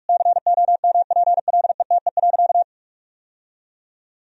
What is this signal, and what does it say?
Morse code, heard as a keyed tone, about 46 wpm, 700 Hz. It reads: COMPLETE.
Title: Sentences from Top 1000 Words - Review Part 2 - 45wpm